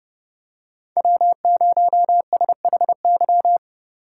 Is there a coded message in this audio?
Yes: Morse code from 1.0 to 3.6 s, heard as a keyed tone, about 29 wpm, 700 Hz. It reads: W0SHY